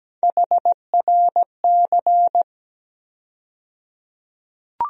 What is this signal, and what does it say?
Morse code, heard as a keyed tone, about 17 wpm, 700 Hz. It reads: HRC
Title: USA Call Sign Postfixes - 3 Letters 17wpm